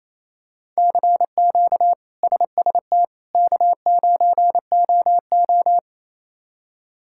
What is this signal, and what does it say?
Morse code, heard as a keyed tone, about 28 wpm, 700 Hz. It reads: CQ SST K9OO
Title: K1USN SST QSO Practice - 28wpm